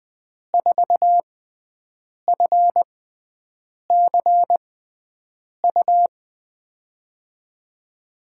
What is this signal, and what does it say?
Morse code, heard as a keyed tone, about 20 wpm, spaced as for 7 wpm, 700 Hz. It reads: 4FCU
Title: USA Call Signs - 6 Characters - 6x Character Spacing - 20wpm